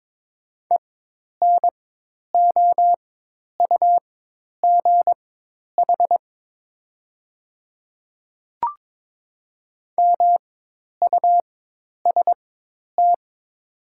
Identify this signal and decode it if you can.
Morse code, heard as a keyed tone, about 22 wpm, spaced as for 10 wpm, 700 Hz. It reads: ENOUGH MUST
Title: 4x Character Spacing - Top 500 Words - 22wpm